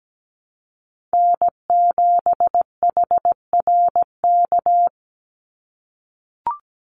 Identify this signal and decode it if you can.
Morse code, heard as a keyed tone, about 17 wpm, 700 Hz. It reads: N7HRK
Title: Call Signs - 17wpm